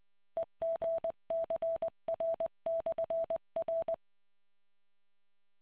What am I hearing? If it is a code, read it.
Morse code, heard as a keyed tone, about 20 wpm, 660 Hz. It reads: EGCR/R